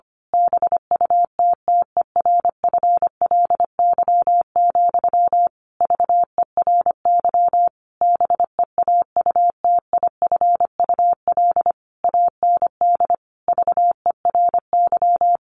Morse code, heard as a keyed tone, about 25 wpm, 700 Hz. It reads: BUTTERFLY, VERY BEAUTIFUL AND VERY
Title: A Sound Of Thunder 25wpm